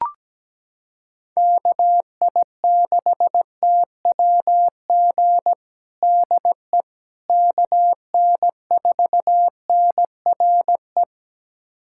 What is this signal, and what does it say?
Morse code, heard as a keyed tone, about 17 wpm, 700 Hz. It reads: KI6TWG DE KN4NRE
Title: USA Callsign DE Callsign - 6 Characters - 17wpm